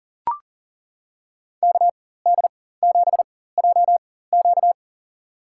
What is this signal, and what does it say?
Morse code, heard as a keyed tone, about 40 wpm, spaced as for 19 wpm, 700 Hz. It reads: KD7JQ
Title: USA Call Signs - 5 Characters - 4x Character Spacing - 40wpm